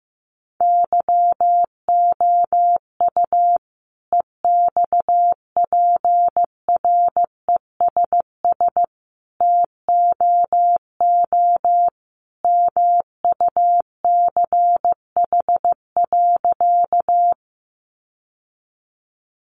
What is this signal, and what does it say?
Morse code, heard as a keyed tone, about 15 wpm, 700 Hz. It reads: YOU EXPRESS TOO MUCH.
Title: Sentences from Top 1100 Words 15wpm